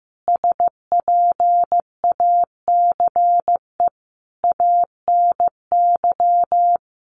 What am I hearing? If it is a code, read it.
Morse code, heard as a keyed tone, about 15 wpm, 700 Hz. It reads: SPACE ANY
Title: Sentences from Top 1500 Words 15wpm